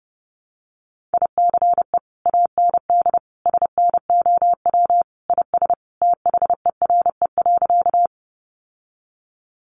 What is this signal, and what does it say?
Morse code, heard as a keyed tone, about 30 wpm, 700 Hz. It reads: ICE AND SNOW IS THERE.
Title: Sentences from Top 1000 Words 30wpm